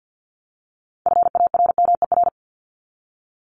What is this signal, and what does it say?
Morse code, heard as a keyed tone, about 50 wpm, 700 Hz. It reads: FARMER
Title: Top 1751-2000 Words 50wpm